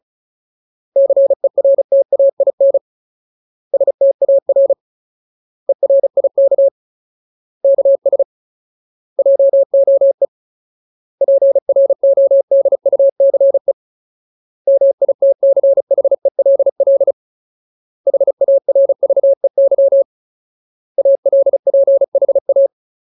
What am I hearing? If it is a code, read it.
Morse code, heard as a keyed tone, about 34 wpm, 550 Hz. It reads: CERTAIN STAR ERIK KS JOE PRODUCE MITCHELL HARVEY ALPHA
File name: Ultimate Word List / 35WPM Ultimate